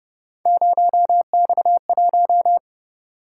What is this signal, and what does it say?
Morse code, heard as a keyed tone, about 31 wpm, 700 Hz. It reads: 0X1